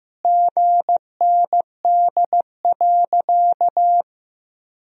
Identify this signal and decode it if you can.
Morse code, heard as a keyed tone, about 15 wpm, 700 Hz. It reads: GND.
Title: Sentences from Top 500 Words and Abbreviations 15wpm